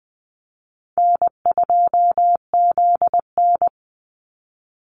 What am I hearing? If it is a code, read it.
Morse code, heard as a keyed tone, about 20 wpm, 700 Hz. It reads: N2ZN
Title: Easy Call Signs - 20wpm